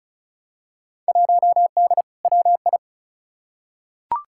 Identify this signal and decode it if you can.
Morse code, heard as a keyed tone, about 36 wpm, 700 Hz. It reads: 1D WI
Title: Field Day Exchanges - 35wpm